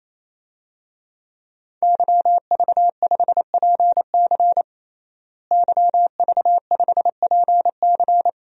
Morse code, heard as a keyed tone, about 28 wpm, 700 Hz. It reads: YV5PC YV5PC
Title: DXCC-100 Call Signs - 5 Characters - Repeated 3x - 28wpm